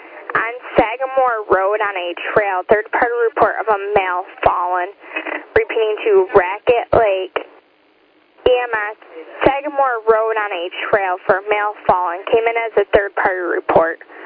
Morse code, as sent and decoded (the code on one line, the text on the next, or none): none